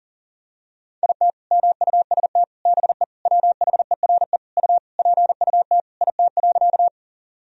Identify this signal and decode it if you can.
Morse code, heard as a keyed tone, about 37 wpm, 700 Hz. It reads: IT MUST BE WHERE U PUT IT.
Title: Sentences from Top 200 Words and Abbreviations 40wpm